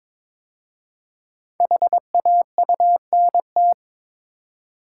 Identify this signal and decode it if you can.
Morse code, heard as a keyed tone, about 22 wpm, 700 Hz. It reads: HAUNT
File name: 5 Letter Words - First 3 Letters Given - 22wpm